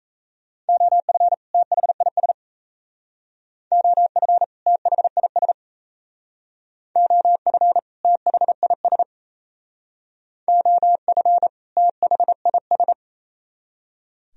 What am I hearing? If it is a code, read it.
Morse code, heard as a keyed tone, about 32 wpm, 700 Hz. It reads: OF THIS OF THIS OF THIS OF THIS
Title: Speed-Racing - Sets of 2 Words 28wpm